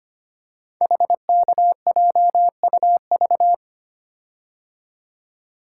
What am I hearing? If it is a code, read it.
Morse code, heard as a keyed tone, about 25 wpm, 700 Hz. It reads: HKJUV
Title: Code Groups - 5 Letters - 25wpm